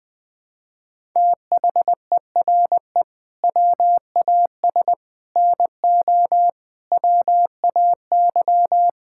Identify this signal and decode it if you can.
Morse code, heard as a keyed tone, about 21 wpm, 700 Hz. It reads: THERE WAS NO WAY
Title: Sets of 8 Words - 20wpm